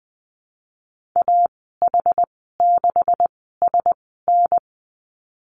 Morse code, heard as a keyed tone, about 20 wpm, spaced as for 15 wpm, 700 Hz. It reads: AH6SN
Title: 161 Lesson 26g - Letter g - Call Signs 20wpm